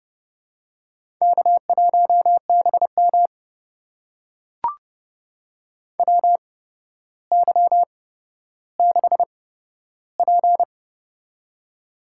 Morse code, heard as a keyed tone, about 30 wpm, spaced as for 13 wpm, 700 Hz. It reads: K1BM WY6P
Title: USA Call Signs - 8x Character Spacing - 30wpm